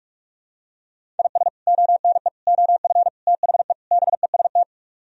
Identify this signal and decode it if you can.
Morse code, heard as a keyed tone, about 48 wpm, 700 Hz. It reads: IS ONE OF THE BEST